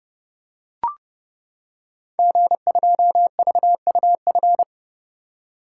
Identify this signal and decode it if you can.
Morse code, heard as a keyed tone, about 30 wpm, 700 Hz. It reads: G2VUF
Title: Code Groups - 5 Letter:Numbers - 30wpm